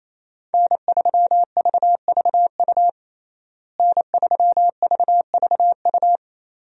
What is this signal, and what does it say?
Morse code, heard as a keyed tone, about 28 wpm, 700 Hz. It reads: N3VVU N3VVU
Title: USA Call Signs - 5 Characters - Repeated 3x - 28wpm